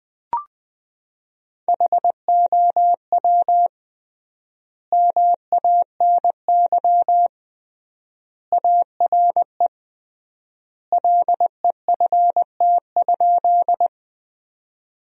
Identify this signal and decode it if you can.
Morse code, heard as a keyed tone, about 21 wpm, 700 Hz. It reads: HOW MANY ARE LEFT?
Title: 3x Word-Spacing - Sentences from Top 300 Words - 20wpm